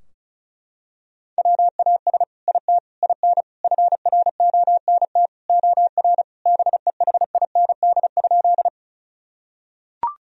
Morse code, heard as a keyed tone, about 34 wpm, 700 Hz. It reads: WAS IT IN FRONT OR BEHIND?
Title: Sentences from Top 1000 Words - Review Part 2 - 35wpm